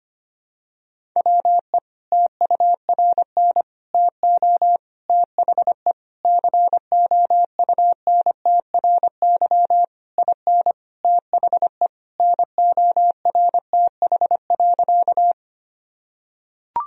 Morse code, heard as a keyed tone, about 25 wpm, 700 Hz. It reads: WE TURN TO THE COUNTRY IN THE NORTH.
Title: Sentences from Top 1000 Words - Review Part 1 - 25wpm